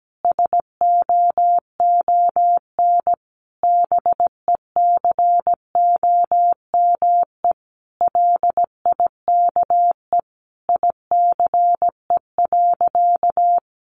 Morse code, heard as a keyed tone, about 17 wpm, 700 Hz. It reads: SOON BECOME LIKE ICE.
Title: Sentences from Top 1000 Words 17wpm